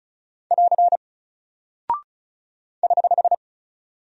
Morse code, heard as a keyed tone, about 36 wpm, 700 Hz. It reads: <AR> <HH>
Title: Single Character (Letter:Number:Punctuation:Prosign) Alt 35wpm